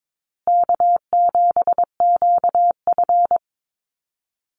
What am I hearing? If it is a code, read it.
Morse code, heard as a keyed tone, about 22 wpm, 700 Hz. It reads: K7QF
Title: USA Call Signs Repeated 2x 22wpm